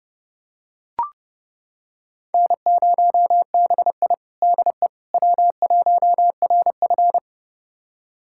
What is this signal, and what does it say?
Morse code, heard as a keyed tone, about 30 wpm, 700 Hz. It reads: N0BI DE W1RF
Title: USA Callsign DE Callsign - 4 Characters - 30wpm